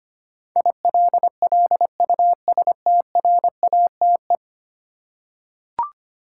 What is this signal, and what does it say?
Morse code, heard as a keyed tone, about 25 wpm, 700 Hz. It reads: ILLUSTRATE
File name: Ten Letter Words - 25wpm